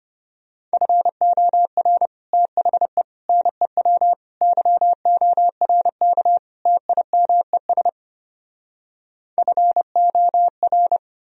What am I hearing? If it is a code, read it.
Morse code, heard as a keyed tone, about 33 wpm, 700 Hz. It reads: FOR THE NEW YORK TIMES FOR